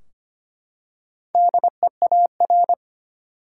Morse code, heard as a keyed tone, about 25 wpm, 700 Hz. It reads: DEAR
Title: Four Letter Words - 25wpm